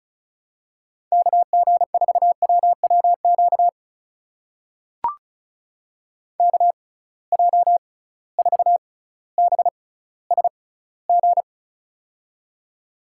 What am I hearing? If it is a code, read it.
Morse code, heard as a keyed tone, about 36 wpm, spaced as for 18 wpm, 700 Hz. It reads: KG4WWQ KJ4BSG
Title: USA Call Signs - 6 Characters - 6x Character Spacing - 35wpm